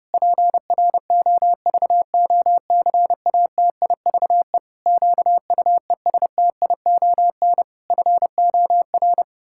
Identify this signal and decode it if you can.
Morse code, heard as a keyed tone, about 30 wpm, 700 Hz. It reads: PROVOCATIVE QUESTION FOR
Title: Pangram Challenge v2 - 30wpm